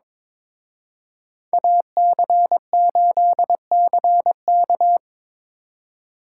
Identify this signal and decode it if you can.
Morse code, heard as a keyed tone, about 25 wpm, 700 Hz. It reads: AC8CK
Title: Speed Racing - USA Call Signs 22wpm